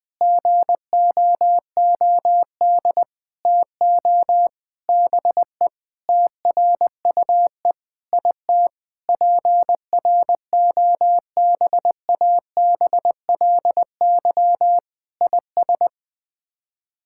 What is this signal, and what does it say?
Morse code, heard as a keyed tone, about 21 wpm, 700 Hz. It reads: GOOD TO BE TRUE IT PROBABLY IS
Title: Sets of 9 Words - 20wpm